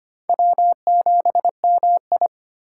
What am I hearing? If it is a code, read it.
Morse code, heard as a keyed tone, about 34 wpm, 700 Hz. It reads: W7MI